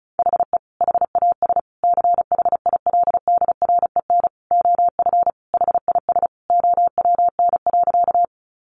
Morse code, heard as a keyed tone, about 34 wpm, 700 Hz. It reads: HE HAS CHILDREN OF HIS OWN.
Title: Sentences from Top 300 Words 35wpm